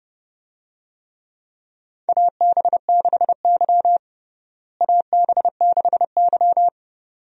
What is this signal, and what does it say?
Morse code, heard as a keyed tone, about 30 wpm, 700 Hz. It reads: AB6Y AB6Y
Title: USA Call Signs - 4 Characters - Repeated 2x - 30wpm